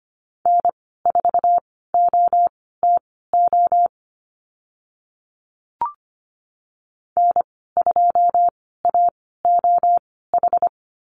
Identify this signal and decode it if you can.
Morse code, heard as a keyed tone, about 25 wpm, spaced as for 16 wpm, 690 Hz. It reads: N4OTO N2AOH
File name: Lesson 15g - Number 5 - Call Signs 25wpm